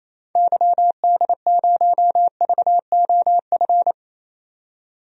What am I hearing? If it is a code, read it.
Morse code, heard as a keyed tone, about 27 wpm, 700 Hz. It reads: YD0VOF
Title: DXCC-100 Call Signs - 6 Characters - 28wpm